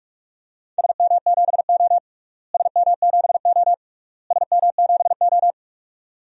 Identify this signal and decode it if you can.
Morse code, heard as a keyed tone, about 45 wpm, 700 Hz. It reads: SM7O SM7O SM7O
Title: DXCC-100 Call Signs - 4 Characters - Repeated 3x - 45wpm